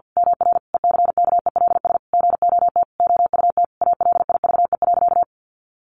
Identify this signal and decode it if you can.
Morse code, heard as a keyed tone, about 50 wpm, 700 Hz. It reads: MR ECKELS GOT OUT ALIVE.
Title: A Sound Of Thunder 50wpm